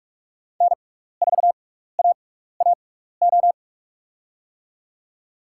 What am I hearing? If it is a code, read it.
Morse code, heard as a keyed tone, about 44 wpm, spaced as for 15 wpm, 690 Hz. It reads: N4AAO
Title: USA Call Signs - 5 Characters - 6x Character Spacing - 45wpm